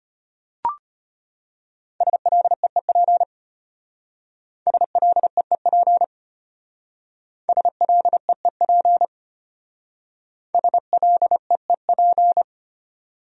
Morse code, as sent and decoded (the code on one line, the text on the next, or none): ... .-.. . .--. / ... .-.. . .--. / ... .-.. . .--. / ... .-.. . .--.
SLEEP SLEEP SLEEP SLEEP